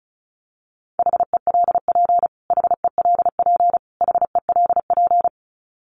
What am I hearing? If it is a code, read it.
Morse code, heard as a keyed tone, about 34 wpm, 700 Hz. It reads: HELP HELP HELP
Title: Top 500 Words Repeated 35wpm